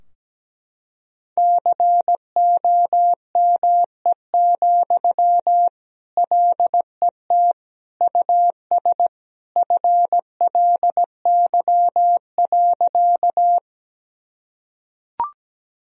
Morse code, heard as a keyed tone, about 17 wpm, 700 Hz. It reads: COME, LET US FLY.
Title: Sentences from Top 1000 Words - Review Part 2 - 17wpm